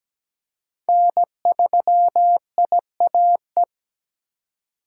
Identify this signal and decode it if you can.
Morse code, heard as a keyed tone, about 17 wpm, 700 Hz. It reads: N3IAE
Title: USA Call Signs - 5 Characters - 2x Character Spacing - 17wpm